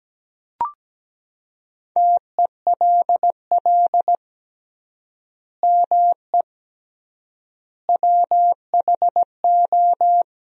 Morse code, heard as a keyed tone, about 17 wpm, 700 Hz. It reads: TELL ME WHO
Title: 3x Word-Spacing - Sentences from Top 200 Words - 17wpm